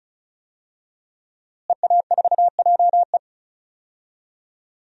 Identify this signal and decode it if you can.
Morse code, heard as a keyed tone, about 34 wpm, 700 Hz. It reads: EA4JE
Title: DXCC-100 Call Signs - 5 Characters - Repeated 2x - 35wpm